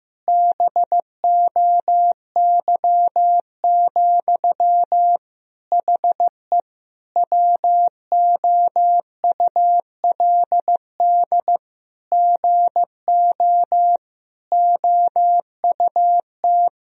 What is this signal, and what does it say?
Morse code, heard as a keyed tone, about 15 wpm, 700 Hz. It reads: BOY, HE WOULD GO OUT
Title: Sentences from Top 1000 Words - Review Part 3 - 15wpm